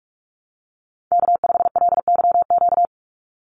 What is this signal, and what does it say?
Morse code, heard as a keyed tone, about 44 wpm, 700 Hz. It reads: K5LYQ
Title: USA Call Signs 45wpm